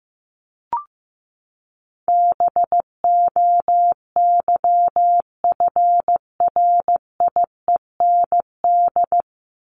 Morse code, heard as a keyed tone, about 15 wpm, 700 Hz. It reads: BOYFRIEND
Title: Top 501-750 Words 15wpm